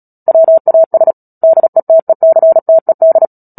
Morse code, heard as a keyed tone, about 37 wpm, 650 Hz. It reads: WAS DETECTED
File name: Book Two / 10-Chapter 10 - Epilogue - 36 WPM